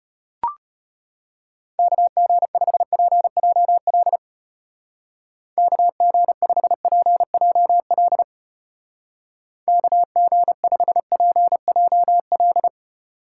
Morse code, heard as a keyed tone, about 31 wpm, 700 Hz. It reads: KG5PJL KG5PJL KG5PJL
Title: Speed Racing - USA 2x3 Call Signs 25wpm